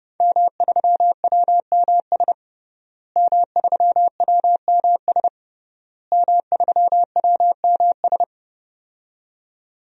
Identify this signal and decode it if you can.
Morse code, heard as a keyed tone, about 32 wpm, 700 Hz. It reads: M3WMS M3WMS M3WMS
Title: DXCC-100 Call Signs - 5 Characters - Repeated 3x - 30wpm